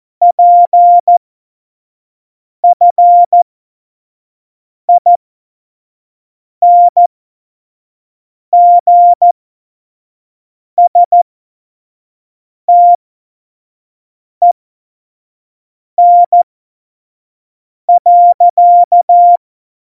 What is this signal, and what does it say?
Morse code, heard as a keyed tone, about 14 wpm, spaced as for 5 wpm, 700 Hz. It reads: PFINGSTEN.
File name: Morsekurs / Witze / Witz 21 April 2020, 25 F70